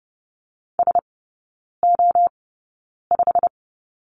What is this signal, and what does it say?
Morse code, heard as a keyed tone, about 30 wpm, spaced as for 9 wpm, 700 Hz. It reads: SO5
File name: course / 30:10wpm / 087 Lesson 15e - Number 5 - Three Characters 30wpm